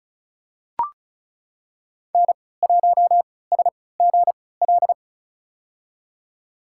Morse code, heard as a keyed tone, about 36 wpm, spaced as for 20 wpm, 700 Hz. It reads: N1SGL
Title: USA Call Signs - 5 Characters - 3x Character Spacing - 35wpm